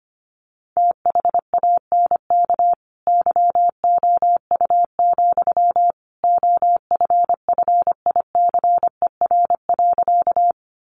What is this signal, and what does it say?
Morse code, heard as a keyed tone, about 25 wpm, 700 Hz. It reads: THANK YOU, OFFICER.
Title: Sentences from Top 1200 Words 25wpm